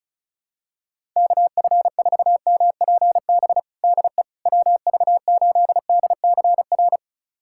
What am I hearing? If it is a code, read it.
Morse code, heard as a keyed tone, about 36 wpm, 700 Hz. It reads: KF4MPB DE WV8DCR
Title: USA Callsign DE Callsign 35wpm